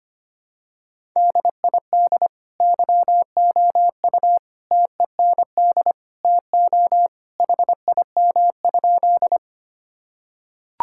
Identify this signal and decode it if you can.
Morse code, heard as a keyed tone, about 25 wpm, 700 Hz. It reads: DID YOU TEND TO HIM?
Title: Sentences from Top 1000 Words - Review Part 5 - 25wpm